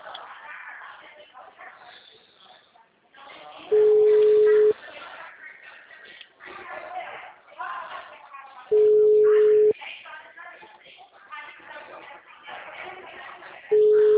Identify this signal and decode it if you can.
Ringback tone (Europe, 425 Hz), with speech.